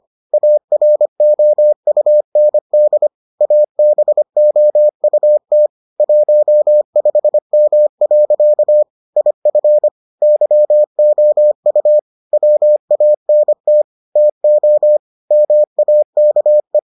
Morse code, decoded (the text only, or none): AROUND ABOUT 15M. IF YOU WANT TO MAKE